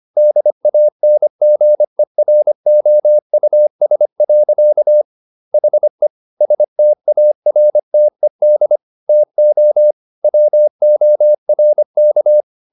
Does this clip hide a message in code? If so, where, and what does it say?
Yes: Morse code from 0.2 to 12.4 s, heard as a keyed tone, about 25 wpm, 600 Hz. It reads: DANGEROUS. HE STARTED TO WORK